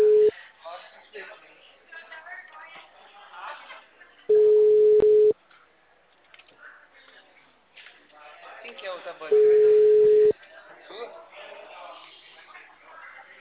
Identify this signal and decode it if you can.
Ringback tone (Europe, 425 Hz), with speech.
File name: out- -2012- - - .12767